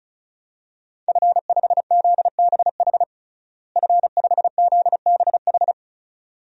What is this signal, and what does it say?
Morse code, heard as a keyed tone, about 34 wpm, 700 Hz. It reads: F5ZBH F5ZBH
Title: DXCC-100 Call Signs - 5 Characters - Repeated 2x - 35wpm